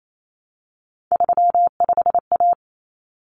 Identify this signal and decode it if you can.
Morse code, heard as a keyed tone, about 28 wpm, 700 Hz. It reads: 35A
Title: Common Cut Numbers - 3-Digit 28wpm